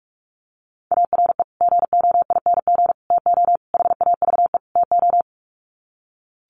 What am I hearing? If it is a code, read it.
Morse code, heard as a keyed tone, about 44 wpm, 700 Hz. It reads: ARE GOING TO HAVE TO
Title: Sets of 5 Words 45wpm